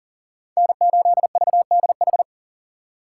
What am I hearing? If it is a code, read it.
Morse code, heard as a keyed tone, about 40 wpm, 700 Hz. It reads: N8VDH
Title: USA Call Signs - 5 Characters - 2x Character Spacing - 40wpm